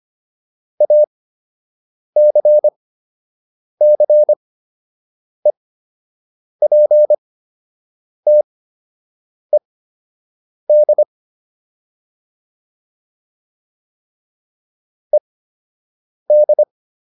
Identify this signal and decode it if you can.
Morse code, heard as a keyed tone, about 25 wpm, spaced as for 7 wpm, 600 Hz. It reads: ACCEPTED ED